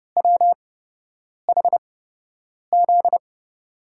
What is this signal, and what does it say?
Morse code, heard as a keyed tone, about 31 wpm, spaced as for 8 wpm, 700 Hz. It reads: WHZ
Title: USA Call Signs - 8x Character Spacing - 30wpm